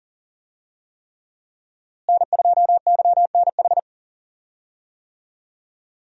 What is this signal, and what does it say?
Morse code, heard as a keyed tone, about 37 wpm, 700 Hz. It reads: N2YNH
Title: USA Call Signs - 5 Characters - Repeated 3x - 40wpm